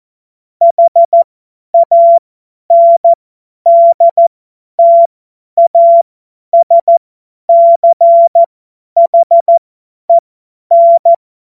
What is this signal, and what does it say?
Morse code, heard as a keyed tone, about 14 wpm, spaced as for 10 wpm, 680 Hz. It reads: HANDTASCHEN
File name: Morsekurs / Witze / Witz 30. August 2022 50 F70